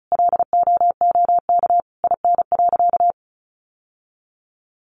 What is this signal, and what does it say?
Morse code, heard as a keyed tone, about 34 wpm, 700 Hz. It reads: LOOK IN.
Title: Sentences from Top 1300 Words 35wpm